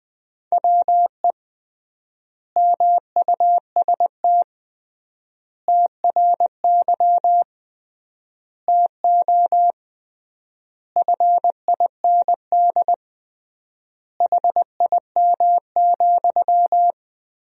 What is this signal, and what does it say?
Morse code, heard as a keyed tone, about 20 wpm, 700 Hz. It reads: WE MUST TRY TO FIND HIM,
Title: 3x Word-Spacing - Sentences from Top 400 Words - 20wpm